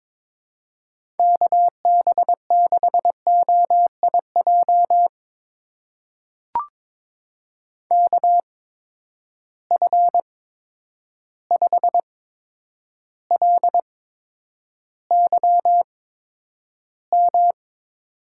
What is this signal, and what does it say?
Morse code, heard as a keyed tone, about 22 wpm, spaced as for 9 wpm, 700 Hz. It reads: KB6OIJ KF5LYM
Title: USA Call Signs - 6 Characters - 8x Character Spacing - 22wpm